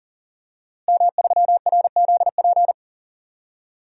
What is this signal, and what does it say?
Morse code, heard as a keyed tone, about 40 wpm, 700 Hz. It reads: M3RZP
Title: DXCC-100 Call Signs - 5 Characters - Repeated 2x - 40wpm